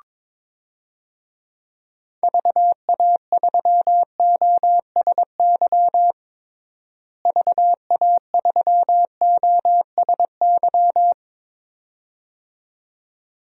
Morse code, heard as a keyed tone, about 22 wpm, 700 Hz. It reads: VA3OSY VA3OSY
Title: DXCC-100 Call Signs - 6 Characters - Repeated 2x - 22wpm